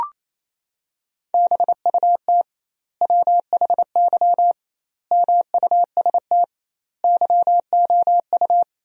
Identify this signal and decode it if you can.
Morse code, heard as a keyed tone, about 27 wpm, 700 Hz. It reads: BUT WHY MUST YOU
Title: 2x Word-Spacing - Sentences from Top 400 Words - 28wpm